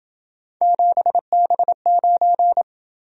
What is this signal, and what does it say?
Morse code, heard as a keyed tone, about 27 wpm, 700 Hz. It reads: 7B9